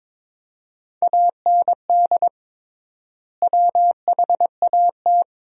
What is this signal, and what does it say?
Morse code, heard as a keyed tone, about 21 wpm, 700 Hz. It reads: AND WHAT
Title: 3x Word-Spacing - Sentences from Top 100 Words - 22wpm